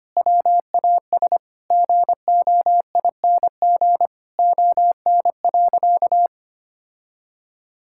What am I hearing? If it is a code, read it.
Morse code, heard as a keyed tone, about 25 wpm, 700 Hz. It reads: WAS GOING ON.
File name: practice / 25wpm / Sentences with 4-Word Phrases - 25wpm